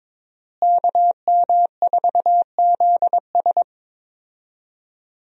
Morse code, heard as a keyed tone, about 22 wpm, 700 Hz. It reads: KM4ZS
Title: USA Call Signs - 5 Characters - Repeated 2x - 22wpm